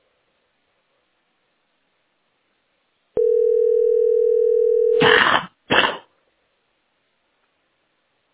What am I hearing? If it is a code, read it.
Ringback tone (North America), 440 + 480 Hz.